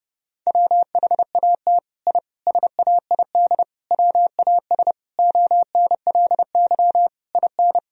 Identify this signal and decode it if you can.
Morse code, heard as a keyed tone, about 32 wpm, 700 Hz. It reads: WHAT I SAID WAS ONLY IN
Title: Sentences from Top 1000 Words - Review Part 4 - 30wpm